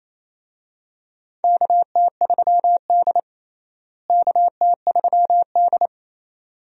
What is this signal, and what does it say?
Morse code, heard as a keyed tone, about 28 wpm, 700 Hz. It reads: KT3D KT3D
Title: USA Call Signs - 4 Characters - Repeated 2x - 28wpm